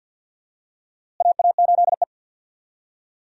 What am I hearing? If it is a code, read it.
Morse code, heard as a keyed tone, about 49 wpm, 700 Hz. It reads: AA8E
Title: USA Call Signs - 4 Characters - Repeated 2x - 50wpm